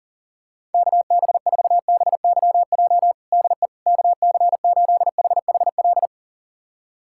Morse code, heard as a keyed tone, about 40 wpm, 700 Hz. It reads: KB4BYJ DE KC8HHL
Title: USA Callsign DE Callsign 40wpm